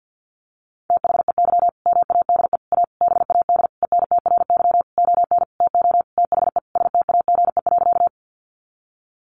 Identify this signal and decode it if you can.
Morse code, heard as a keyed tone, about 50 wpm, 700 Hz. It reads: THEY MADE A BAD ENTRY ON TO THE STAGE.
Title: Sentences from Top 1500 Words 50wpm